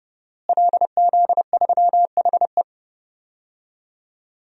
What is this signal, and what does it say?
Morse code, heard as a keyed tone, about 30 wpm, 700 Hz. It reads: LZ3HE
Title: DXCC-100 Call Signs - 5 Characters - Repeated 2x - 30wpm